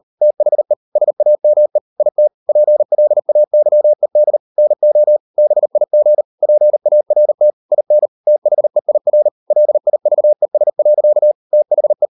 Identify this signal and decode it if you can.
Morse code, heard as a keyed tone, about 38 wpm, 600 Hz. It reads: THE SAME IT PLAYED NO BIG PART IN THEIR LIVES. THE